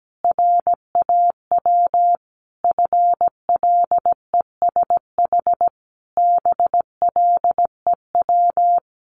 Morse code, heard as a keyed tone, about 17 wpm, 700 Hz. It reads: RAW FLESH BLEW
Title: A Sound Of Thunder 17wpm